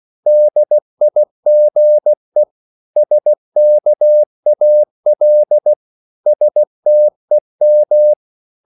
Morse code, heard as a keyed tone, about 16 wpm, 600 Hz. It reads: DIGE SKAL STEM
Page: Korte danske ord 16wpm